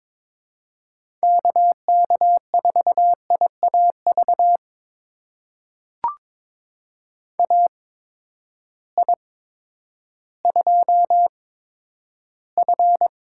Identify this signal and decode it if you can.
Morse code, heard as a keyed tone, about 22 wpm, spaced as for 11 wpm, 700 Hz. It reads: KK4IAV AI2F
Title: USA Call Signs - 8x Character Spacing - 22wpm